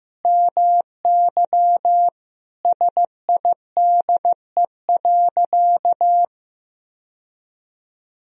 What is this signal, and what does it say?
Morse code, heard as a keyed tone, about 15 wpm, 700 Hz. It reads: MY SIDE.
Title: Sentences from Top 1000 Words - Review Part 1 - 15wpm